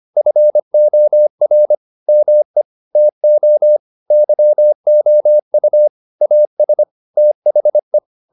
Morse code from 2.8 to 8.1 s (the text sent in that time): TO YOU AS THE